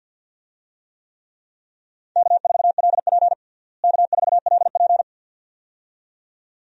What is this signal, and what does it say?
Morse code, heard as a keyed tone, about 52 wpm, 700 Hz. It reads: K4LP K4LP
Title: USA Call Signs - 4 Characters - Repeated 2x - 50wpm